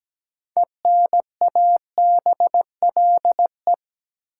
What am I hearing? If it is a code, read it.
Morse code, heard as a keyed tone, about 17 wpm, 700 Hz. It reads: ENABLE
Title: Six Letter Words - 17wpm